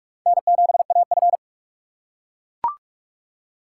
Morse code, heard as a keyed tone, about 44 wpm, 700 Hz. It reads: N7AF
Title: USA Call Signs - 4 Characters - 6x Character Spacing - 45wpm